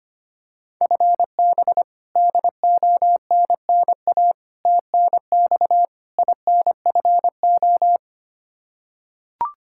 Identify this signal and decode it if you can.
Morse code, heard as a keyed tone, about 25 wpm, 700 Hz. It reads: FB DONNA TNX INFO
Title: Practice QSO - Exchange 3 - More Info - 25wpm